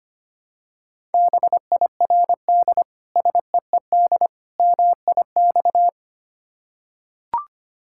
Morse code, heard as a keyed tone, about 25 wpm, 700 Hz. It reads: BIRD SEED MIX
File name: practice / 25wpm / Difficult Words with Hints - 25wpm